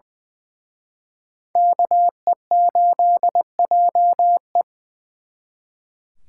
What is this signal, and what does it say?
Morse code, heard as a keyed tone, about 23 wpm, 700 Hz. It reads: KE8JE
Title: Speed Racing - USA 2x2 Call Signs 20wpm